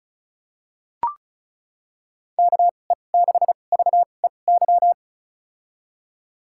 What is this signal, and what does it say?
Morse code, heard as a keyed tone, about 36 wpm, spaced as for 25 wpm, 700 Hz. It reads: KE6VEY
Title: USA Call Signs - 6 Characters - 2x Character Spacing - 35wpm